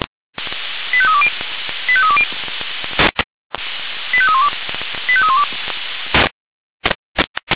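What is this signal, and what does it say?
Five-tone selective call (ZVEI), heard as quick runs of five tones, tones of about 70 ms. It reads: 85320 85320 85312 85312
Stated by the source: ZVEI five-tone selective call received on SDR.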